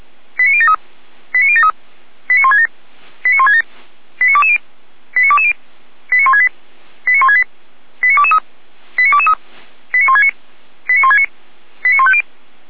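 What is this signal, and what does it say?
Five-tone selective call (ZVEI), heard as quick runs of five tones, tones of about 70 ms. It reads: 89073 89073 89167 89167 89220 89220 89168 89168 89223 89223 89169 89169 89150
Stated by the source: ZVEI five-tone selective call 89022, received by SDR, Feuerwehr Orlamünde.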